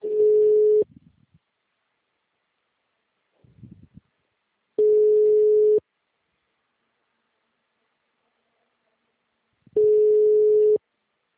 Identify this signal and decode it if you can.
Ringback tone (Europe, 425 Hz).